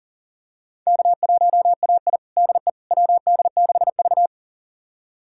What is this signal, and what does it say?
Morse code, heard as a keyed tone, about 37 wpm, 700 Hz. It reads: K1AI DE WD6V
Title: USA Callsign DE Callsign - 4 Characters - 40wpm